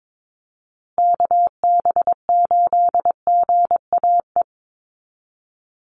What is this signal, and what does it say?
Morse code, heard as a keyed tone, about 22 wpm, 700 Hz. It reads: KB8GAE
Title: Call Signs Repeated 2x - 22wpm